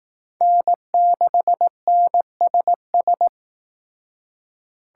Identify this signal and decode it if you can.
Morse code, heard as a keyed tone, about 18 wpm, 700 Hz. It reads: N6NSS
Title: Speed Racing - USA 1x3 Call Signs 15wpm